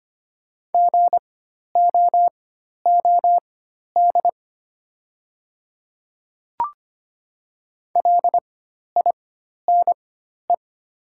Morse code, heard as a keyed tone, about 25 wpm, spaced as for 12 wpm, 700 Hz. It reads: GOOD LINE
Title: 4x Character Spacing - Top 200 Words - 25wpm